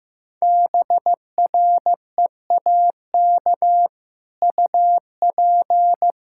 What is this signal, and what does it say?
Morse code, heard as a keyed tone, about 15 wpm, 700 Hz. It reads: BREAK UP
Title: Sentences from Top 1600 Words 15wpm